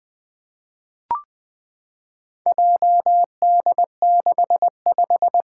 Morse code, heard as a keyed tone, about 21 wpm, 700 Hz. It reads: JD65
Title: Four Letter-Number Combinations - 20wpm